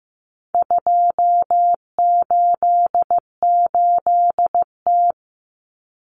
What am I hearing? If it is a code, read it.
Morse code, heard as a keyed tone, about 15 wpm, 700 Hz. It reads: 288T
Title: Common Cut Numbers - 4-Digit 15wpm